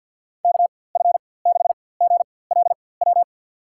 Morse code, heard as a keyed tone, about 52 wpm, spaced as for 23 wpm, 700 Hz. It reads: KF6GLW
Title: USA Call Signs - 6 Characters - 4x Character Spacing - 50wpm